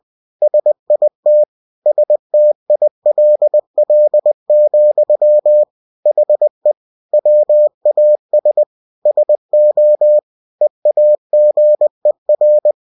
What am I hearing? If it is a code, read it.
Morse code, heard as a keyed tone, about 20 wpm, 600 Hz. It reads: SIT STILL, HE WAS SO EAGER
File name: cw-books / winnie-the-pooh / winnie-20wpm-page-0063